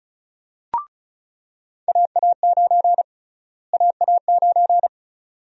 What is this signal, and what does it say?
Morse code, heard as a keyed tone, about 34 wpm, 700 Hz. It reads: AA9 AA9
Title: USA Call Sign Prefixes - Repeated 2x - 35wpm